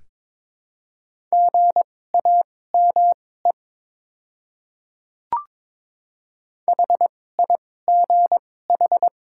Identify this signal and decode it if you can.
Morse code, heard as a keyed tone, about 22 wpm, spaced as for 16 wpm, 700 Hz. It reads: GAME HIGH
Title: 2x Character Spacing - Top 500 Words - 22wpm